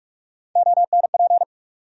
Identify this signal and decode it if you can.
Morse code, heard as a keyed tone, about 47 wpm, 700 Hz. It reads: ONP
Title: USA Call Sign Postfixes - 3 Letters 45wpm